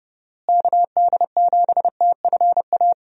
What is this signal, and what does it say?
Morse code, heard as a keyed tone, about 30 wpm, 700 Hz. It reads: KD7TFA
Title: USA Call Signs - 6 Characters - Repeated 3x - 30wpm